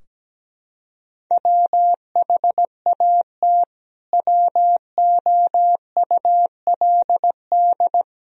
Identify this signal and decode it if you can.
Morse code, heard as a keyed tone, about 17 wpm, 700 Hz. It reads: WHAT WOULD
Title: Sentences from Top 1000 Words - Review Part 3 - 17wpm